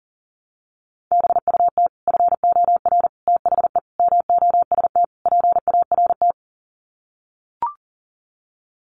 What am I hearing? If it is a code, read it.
Morse code, heard as a keyed tone, about 40 wpm, 700 Hz. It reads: BUT FOR THE MOST PART
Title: Sets of 5 Words Encore 40wpm